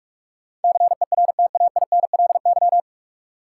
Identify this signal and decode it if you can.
Morse code, heard as a keyed tone, about 44 wpm, 700 Hz. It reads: CERTAINLY
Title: Sentences from Top 900 Words - New Word Drill - 45wpm